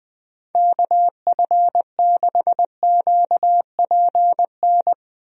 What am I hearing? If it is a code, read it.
Morse code, heard as a keyed tone, about 20 wpm, 700 Hz. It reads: KF6QPN
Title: USA Call Signs - 6 Characters - Repeated 2x - 20wpm